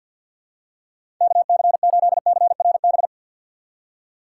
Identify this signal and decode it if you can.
Morse code, heard as a keyed tone, about 52 wpm, 700 Hz. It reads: KC8CRB
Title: USA Call Signs - 6 Characters - 6x Character Spacing - 50wpm